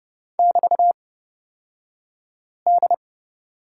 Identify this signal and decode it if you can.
Morse code, heard as a keyed tone, about 30 wpm, spaced as for 5 wpm, 700 Hz. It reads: <BT>D
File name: Lesson 36d - BT - Two Characters 30wpm